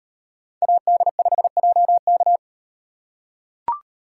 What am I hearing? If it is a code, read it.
Morse code, heard as a keyed tone, about 38 wpm, 700 Hz. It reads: AD5JK